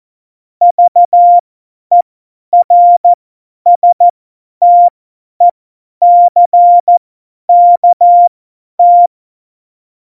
Morse code, heard as a keyed tone, about 14 wpm, spaced as for 10 wpm, 700 Hz. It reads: VERSTECKT
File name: Morsekurs / Witze / Witz 03.Juli 2018, 50 F70